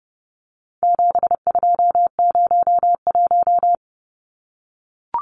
Morse code, heard as a keyed tone, about 30 wpm, 700 Hz. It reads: 7201
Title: Numbers - 4-Digit 30wpm